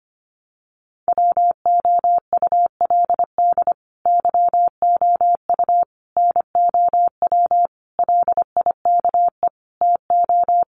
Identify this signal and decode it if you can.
Morse code, heard as a keyed tone, about 25 wpm, 700 Hz. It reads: WOULD YOU NOW LIKE TO
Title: Sentences from Top 200 Words 25wpm